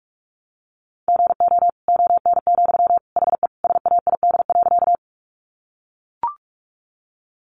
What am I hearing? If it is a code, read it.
Morse code, heard as a keyed tone, about 44 wpm, 700 Hz. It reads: GO ON, HE SAID.
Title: Sentences from Top 100 Words 45wpm